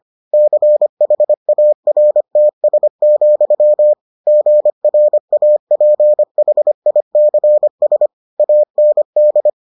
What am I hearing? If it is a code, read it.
Morse code, heard as a keyed tone, about 25 wpm, 600 Hz. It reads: CHARTS, GRAPHICS AND